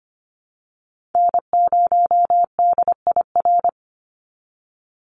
Morse code, heard as a keyed tone, about 25 wpm, 700 Hz. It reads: N0DIR
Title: Call Signs Repeated 2x - 25wpm